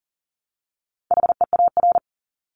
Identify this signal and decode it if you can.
Morse code, heard as a keyed tone, about 40 wpm, 700 Hz. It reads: HEAR
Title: Top 500 Words Repeated 40wpm